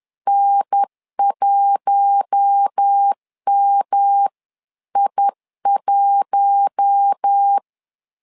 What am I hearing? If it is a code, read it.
Morse code, heard as a keyed tone, about 10 wpm, 800 Hz. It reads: N1MI1
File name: LEZIONI / 4 Q1 10wpm 2448